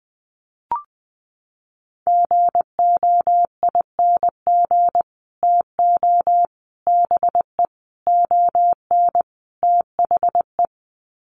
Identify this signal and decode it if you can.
Morse code, heard as a keyed tone, about 20 wpm, 700 Hz. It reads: GOING TO BE ON THE
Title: Sets of 5 Words Encore2 20wpm